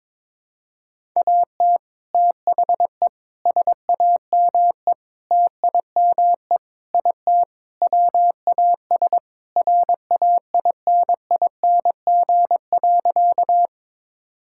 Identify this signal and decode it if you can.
Morse code, heard as a keyed tone, about 21 wpm, 700 Hz. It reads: AT THE SAME TIME IT WAS RAINING.
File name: practice / 22wpm / Sentences with 3-Word Phrases - 22wpm